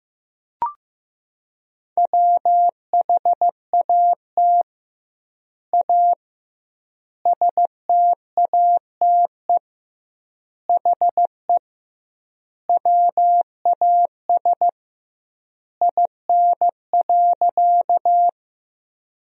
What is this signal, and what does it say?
Morse code, heard as a keyed tone, about 15 wpm, 700 Hz. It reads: WHAT A STATE HE WAS IN.
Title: 2x Word-Spacing - Sentences from Top 400 Words - 15wpm